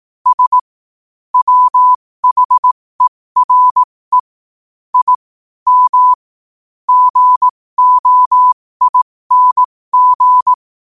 Morse code, heard as a keyed tone, about 18 wpm, spaced as for 15 wpm, 1000 Hz. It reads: S WHERE I M GOING
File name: qotdfiles / 15wpm